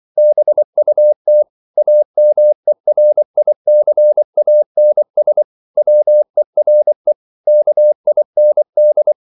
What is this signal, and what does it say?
Morse code, heard as a keyed tone, about 24 wpm, 600 Hz. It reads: BUT AMERICANS WERE KIND